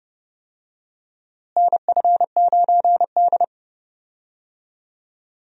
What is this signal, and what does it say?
Morse code, heard as a keyed tone, about 32 wpm, 700 Hz. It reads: NF9D
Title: USA Call Signs - 4 Characters - Repeated 2x - 30wpm